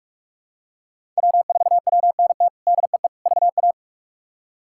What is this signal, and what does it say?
Morse code, heard as a keyed tone, about 41 wpm, 700 Hz. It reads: W4WNT DEE VA
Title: K1USN SST QSO Practice - 45wpm